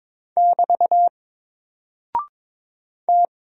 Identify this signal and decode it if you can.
Morse code, heard as a keyed tone, about 22 wpm, 700 Hz. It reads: <BT> T
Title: Single Character (Letter:Number:Punctuation:Prosign) 22wpm